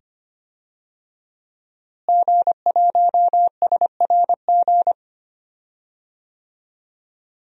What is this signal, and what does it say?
Morse code, heard as a keyed tone, about 25 wpm, 700 Hz. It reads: G1SRG
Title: DXCC-100 Call Signs - 5 Characters - Repeated 2x - 25wpm